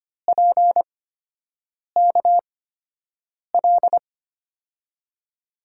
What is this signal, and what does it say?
Morse code, heard as a keyed tone, about 25 wpm, spaced as for 7 wpm, 700 Hz. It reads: PKL
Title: USA Call Signs - 6 Characters - 8x Character Spacing - 25wpm